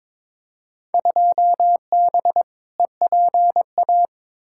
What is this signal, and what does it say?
Morse code, heard as a keyed tone, about 21 wpm, 700 Hz. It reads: 2B EPA
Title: Field Day Exchanges Encore - 22wpm